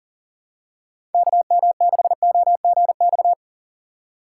Morse code, heard as a keyed tone, about 40 wpm, 700 Hz. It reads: KM6OGX
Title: USA Call Signs - 6 Characters - 2x Character Spacing - 40wpm